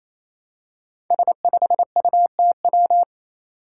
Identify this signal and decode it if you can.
Morse code, heard as a keyed tone, about 28 wpm, 700 Hz. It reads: S5UTW
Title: Five Challenging Letter-Number Combinations 28wpm